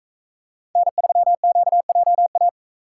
Morse code, heard as a keyed tone, about 44 wpm, 700 Hz. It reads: N3QJA